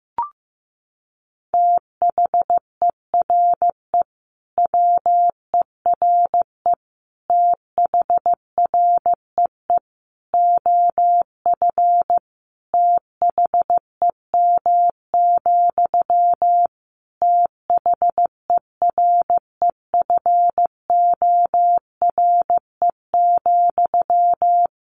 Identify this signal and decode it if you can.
Morse code, heard as a keyed tone, about 15 wpm, 700 Hz. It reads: THERE WERE THREE OF THEM, THEREFORE,
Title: Sentences from Top 800 Words 15wpm